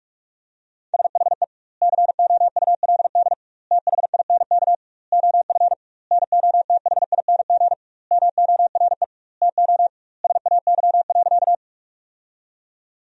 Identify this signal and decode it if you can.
Morse code, heard as a keyed tone, about 42 wpm, 700 Hz. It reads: SHE COULD THINK OF NOTHING MORE TO SAY.
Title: 2x Word-Spacing - Sentences from Top 500 Words - 45wpm